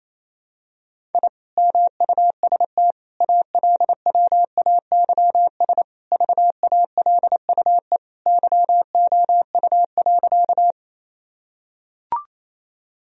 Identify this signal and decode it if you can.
Morse code, heard as a keyed tone, about 28 wpm, 700 Hz. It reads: I MUST ALWAYS VALUE YOU.
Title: Sentences from Top 1000 Words - Review Part 3 - 28wpm